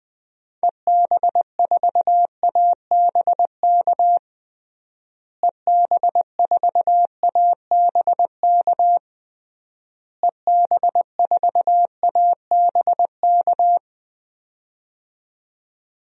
Morse code, heard as a keyed tone, about 19 wpm, 700 Hz. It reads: EB4ABK EB4ABK EB4ABK
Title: DXCC-100 Call Signs - 6 Characters - Repeated 3x - 20wpm